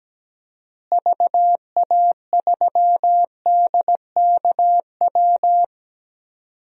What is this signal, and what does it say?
Morse code, heard as a keyed tone, about 17 wpm, 700 Hz. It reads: VA3DKW
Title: DXCC-100 Call Signs - 6 Characters - Repeated 3x - 17wpm